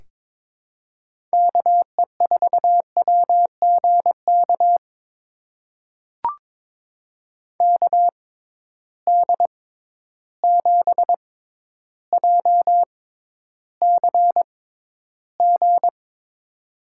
Morse code, heard as a keyed tone, about 23 wpm, spaced as for 11 wpm, 700 Hz. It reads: KE4WGK KD7JCG
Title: USA Call Signs - 6 Characters - 6x Character Spacing - 22wpm